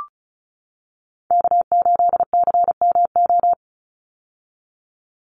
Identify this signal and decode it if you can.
Morse code, heard as a keyed tone, about 34 wpm, 700 Hz. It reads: K8CMO
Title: Call Signs - 35wpm